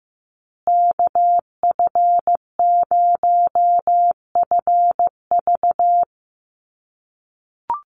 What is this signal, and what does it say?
Morse code, heard as a keyed tone, about 15 wpm, 700 Hz. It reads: KF0FV
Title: USA Call Signs Repeated 3x 15wpm